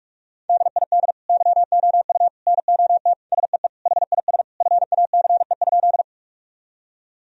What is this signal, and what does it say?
Morse code, heard as a keyed tone, about 46 wpm, 700 Hz. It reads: DID YOU NOT SEE HIS FACE?
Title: Sentences from Top 1000 Words - Review Part 2 - 45wpm